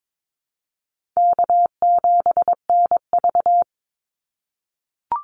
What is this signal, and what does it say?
Morse code, heard as a keyed tone, about 22 wpm, 700 Hz. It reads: K7NV
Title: Easy Call Signs - 22wpm